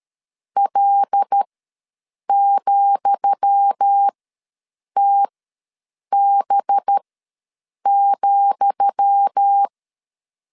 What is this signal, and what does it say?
Morse code, heard as a keyed tone, about 13 wpm, spaced as for 7 wpm, 800 Hz. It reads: L,TB,
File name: LEZIONI / 10 SIGNS / 10D COMA ws 12wpm 3618